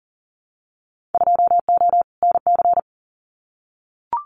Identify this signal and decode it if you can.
Morse code, heard as a keyed tone, about 38 wpm, 700 Hz. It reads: 2O NC
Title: Winter Field Day Exchanges 40wpm